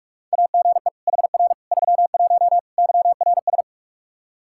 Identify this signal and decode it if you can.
Morse code, heard as a keyed tone, about 46 wpm, 700 Hz. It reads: AGE HR 31 YRS
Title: Practice QSO - Exchange 3 - More Info - 45wpm